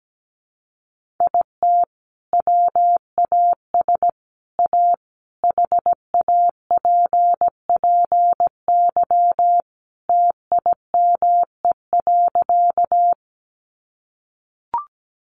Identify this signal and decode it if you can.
Morse code, heard as a keyed tone, about 17 wpm, 700 Hz. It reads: IT WAS A HAPPY TIME.
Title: Sentences from Top 1000 Words 17wpm